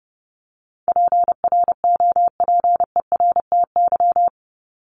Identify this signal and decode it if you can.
Morse code, heard as a keyed tone, about 30 wpm, 700 Hz. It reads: PROPERTY
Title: Top 751-1000 Words 30wpm